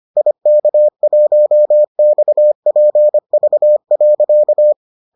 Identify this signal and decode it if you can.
Morse code, heard as a keyed tone, about 25 wpm, 600 Hz. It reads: IK1XPV.